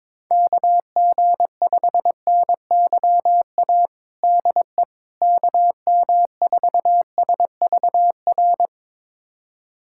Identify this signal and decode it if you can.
Morse code, heard as a keyed tone, about 22 wpm, 700 Hz. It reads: KG5NYA DE KM4SVR
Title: USA Callsign DE Callsign - 6 Characters - 22wpm